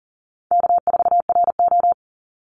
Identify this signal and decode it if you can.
Morse code, heard as a keyed tone, about 39 wpm, 700 Hz. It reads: K4RO
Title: Easy Call Signs Repeated 3x - 40wpm